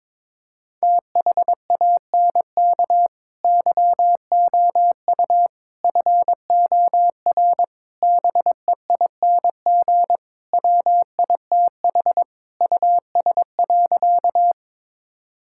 Morse code, heard as a keyed tone, about 22 wpm, 700 Hz. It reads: THANK YOU FOR BEING WITH US.
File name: practice / 22wpm / Sentences with 4-Word Phrases - 22wpm